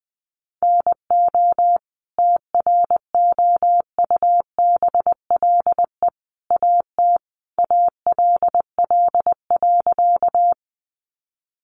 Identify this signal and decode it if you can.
Morse code, heard as a keyed tone, about 20 wpm, 700 Hz. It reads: NO TROUBLE AT ALL.
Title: Sentences from Top 1200 Words 20wpm